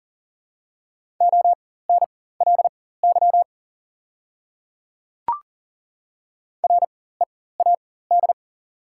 Morse code, heard as a keyed tone, about 42 wpm, spaced as for 19 wpm, 690 Hz. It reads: ONLY READ
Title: 4x Character Spacing - Top 300 Words - 40wpm